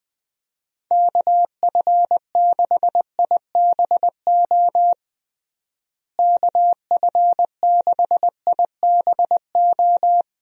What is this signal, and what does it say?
Morse code, heard as a keyed tone, about 21 wpm, 700 Hz. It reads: KF6IBO KF6IBO
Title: USA Call Signs - 6 Characters - Repeated 2x - 20wpm